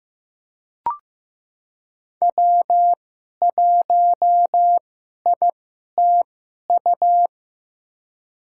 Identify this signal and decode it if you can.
Morse code, heard as a keyed tone, about 15 wpm, spaced as for 11 wpm, 700 Hz. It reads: W1ITU
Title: USA Call Signs - 2x Character Spacing - 15wpm